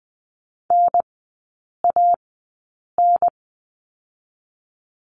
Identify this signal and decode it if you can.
Morse code, heard as a keyed tone, about 20 wpm, 700 Hz. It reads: N A N
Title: 017 Lesson 4e - Letter n - Three Characters 20wpm